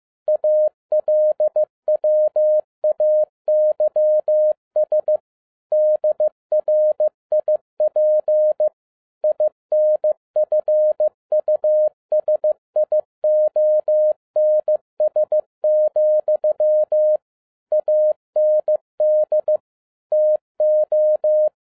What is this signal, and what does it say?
Morse code, heard as a keyed tone, about 15 wpm, 610 Hz. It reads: ALWAYS DRIP INFUSIONS, AND TO
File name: 2011-07-mp3 / n57 BACK FROM A BIG ACCIDENT 15wpm